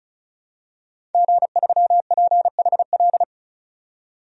Morse code, heard as a keyed tone, about 36 wpm, 700 Hz. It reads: G3PHL
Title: DXCC-100 Call Signs - 5 Characters - 35wpm